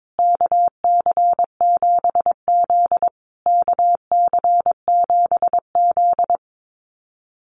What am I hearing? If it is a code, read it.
Morse code, heard as a keyed tone, about 22 wpm, 700 Hz. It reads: KC7Z KC7Z
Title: USA Call Signs Repeated 3x 22wpm